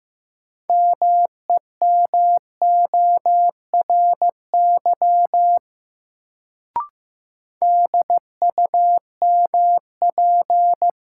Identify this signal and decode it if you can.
Morse code, heard as a keyed tone, about 15 wpm, 700 Hz. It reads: MEMORY DUMP
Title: Sentences from Top 1300 Words - New Word Drill Alt - 15wpm